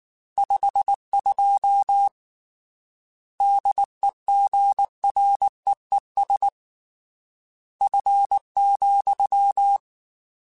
Morse code, heard as a keyed tone, about 19 wpm, 800 Hz. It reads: 52 DEGREES F,
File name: Weather Sets / Weather Sets-mp3s / Weather 4 17 @18